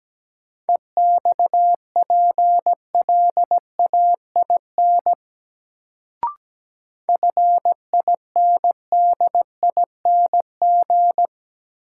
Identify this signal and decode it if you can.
Morse code, heard as a keyed tone, about 17 wpm, 700 Hz. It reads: EXPLAIN FINDING
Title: Sentences from Top 1100 Words - New Word Drill Alt - 17wpm